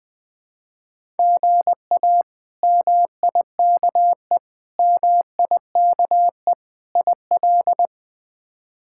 Morse code, heard as a keyed tone, about 20 wpm, 700 Hz. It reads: GA MIKE MIKE IL
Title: K1USN SST QSO Practice - 20wpm